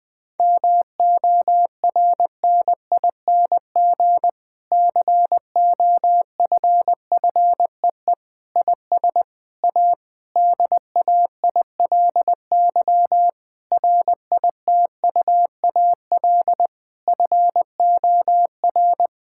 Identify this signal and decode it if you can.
Morse code, heard as a keyed tone, about 20 wpm, 700 Hz. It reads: MORNING COFFEE IS A DAILY RITUAL FOR
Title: Difficult Sentences with Hints - 20wpm